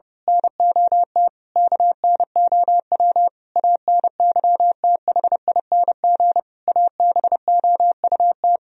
Morse code, heard as a keyed tone, about 28 wpm, 700 Hz. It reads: NOT KNOW ANYTHING ABOUT
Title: Speed-Racing - Sets of 5 Words 30wpm